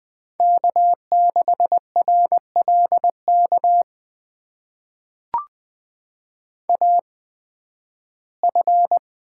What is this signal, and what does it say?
Morse code, heard as a keyed tone, about 20 wpm, spaced as for 13 wpm, 700 Hz. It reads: K6RLK AF
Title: USA Call Signs - 5 Characters - 8x Character Spacing - 20wpm